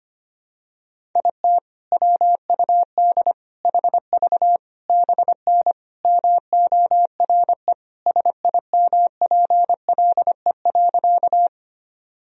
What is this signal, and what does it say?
Morse code, heard as a keyed tone, about 25 wpm, 700 Hz. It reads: IT WUD HV BN MORE SIMPLE.
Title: Sentences from Top 400 Words and Abbreviations 25wpm